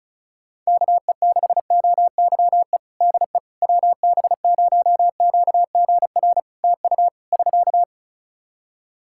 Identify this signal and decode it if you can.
Morse code, heard as a keyed tone, about 36 wpm, 700 Hz. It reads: KE6OYE DE WB0QGR TU <SK>
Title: Practice QSO - Exchange 4 - Ending - 35wpm